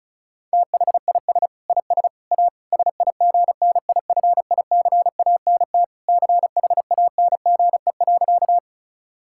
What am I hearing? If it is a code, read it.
Morse code, heard as a keyed tone, about 36 wpm, 700 Hz. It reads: THIS IS A SIGNIFICANT CHANGE.
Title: Sentences from Top 1000 Words - Review Part 6 - 35wpm